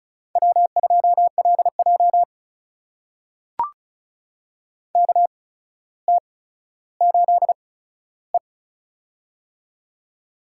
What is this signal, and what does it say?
Morse code, heard as a keyed tone, about 36 wpm, spaced as for 15 wpm, 700 Hz. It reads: W2LJ KT8E
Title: USA Call Signs - 4 Characters - 8x Character Spacing - 35wpm